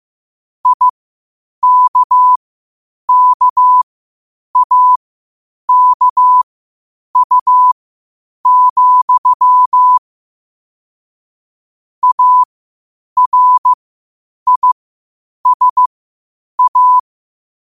Morse code, heard as a keyed tone, about 15 wpm, spaced as for 8 wpm, 1000 Hz. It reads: IKKAKU, ARISA
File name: qotdfiles / 8wpm